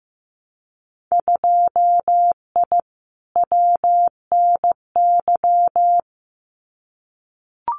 Morse code, heard as a keyed tone, about 15 wpm, 700 Hz. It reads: 2I WNY
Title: Winter Field Day Exchanges Encore 15wpm